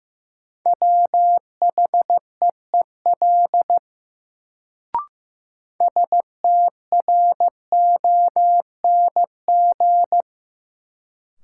Morse code, heard as a keyed tone, about 15 wpm, 700 Hz. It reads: WHEEL STRONG
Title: Sentences from Top 500 Words - New Word Drill - 15wpm